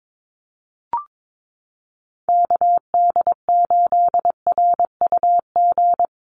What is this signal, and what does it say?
Morse code, heard as a keyed tone, about 22 wpm, 700 Hz. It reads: KD8RUG
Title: USA Call Signs Repeated 2x 22wpm